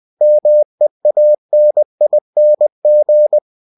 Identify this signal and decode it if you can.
Morse code, heard as a keyed tone, about 20 wpm, 600 Hz. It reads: MEANING
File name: cw-books / winnie-the-pooh / winnie-20wpm-page-0002